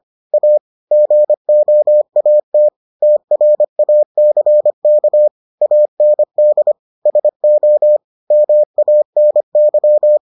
Morse code, heard as a keyed tone, about 25 wpm, 600 Hz. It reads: A GOAT TRACK AND SO MANY